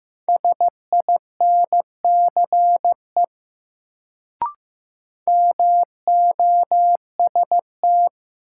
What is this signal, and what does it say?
Morse code, heard as a keyed tone, about 15 wpm, 700 Hz. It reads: SINCE MOST
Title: Sets of 2 Words - New Word Drill - 15wpm